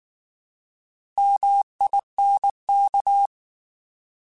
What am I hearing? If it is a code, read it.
Morse code, heard as a keyed tone, about 19 wpm, 800 Hz. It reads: MINK
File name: Alphabetical / Alphabetical mp3s / Ms @18